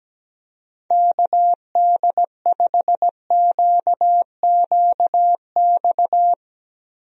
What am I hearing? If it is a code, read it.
Morse code, heard as a keyed tone, about 17 wpm, 700 Hz. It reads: KD5QQX
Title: USA Call Signs - 6 Characters - Repeated 2x - 17wpm